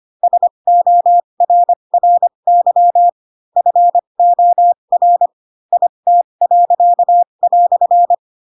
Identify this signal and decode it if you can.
Morse code, heard as a keyed tone, about 25 wpm, spaced as for 22 wpm, 700 Hz. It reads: SORRY FOR IT."
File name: Pinocchio / Pinocchio ch4 18 25 ik1hge